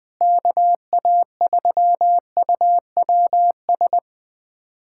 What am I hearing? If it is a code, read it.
Morse code, heard as a keyed tone, about 21 wpm, 700 Hz. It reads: KA3UWS
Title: USA Callsign DE Callsign 20wpm